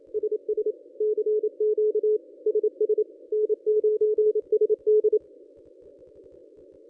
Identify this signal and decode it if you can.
Morse code, heard as a keyed tone, about 27 wpm, 440 Hz. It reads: SS CQ SS N9SD
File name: contests / ss-cw-k5tr-op-n5rz -1000